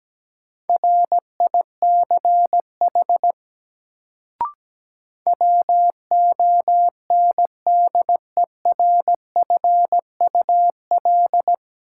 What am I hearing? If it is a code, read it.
Morse code, heard as a keyed tone, about 17 wpm, 700 Hz. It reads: RICH WONDERFUL
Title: Sentences from Top 1100 Words - New Word Drill - 17wpm